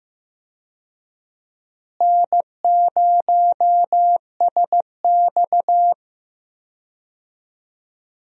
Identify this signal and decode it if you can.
Morse code, heard as a keyed tone, about 15 wpm, 700 Hz. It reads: N0SX
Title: USA Call Signs - 4 Characters - Repeated 3x - 15wpm